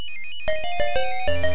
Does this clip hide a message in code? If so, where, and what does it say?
No.